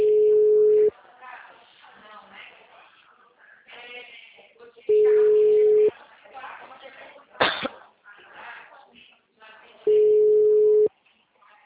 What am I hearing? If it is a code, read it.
Ringback tone (Europe, 425 Hz), with speech.